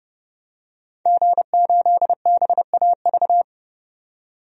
Morse code, heard as a keyed tone, about 30 wpm, 700 Hz. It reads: G8BAV
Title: DXCC-100 Call Signs - 5 Characters - 30wpm